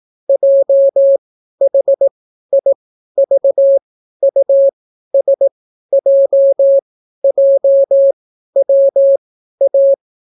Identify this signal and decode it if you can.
Morse code, heard as a keyed tone, about 18 wpm, spaced as for 12 wpm, 550 Hz. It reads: JHIVUSJJWA